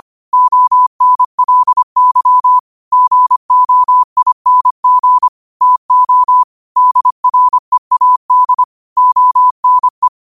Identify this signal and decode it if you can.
Morse code, heard as a keyed tone, about 25 wpm, 1000 Hz. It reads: ONLY GOING TO DREAD ONE
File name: 25wpm